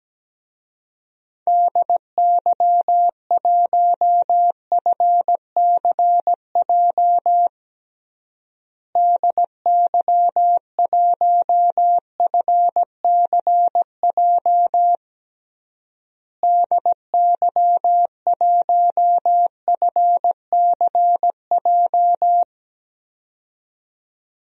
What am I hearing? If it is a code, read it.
Morse code, heard as a keyed tone, about 17 wpm, 700 Hz. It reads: DY1FCJ DY1FCJ DY1FCJ
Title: DXCC-100 Call Signs - 6 Characters - Repeated 3x - 17wpm